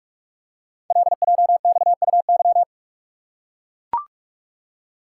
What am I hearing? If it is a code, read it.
Morse code, heard as a keyed tone, about 44 wpm, 700 Hz. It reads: LJXUY
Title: Code Groups - 5 Letter:Numbers - 45wpm